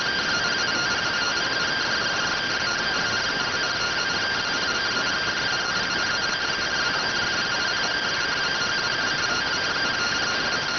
Two-tone teleprinter signal, about 170 Hz shift.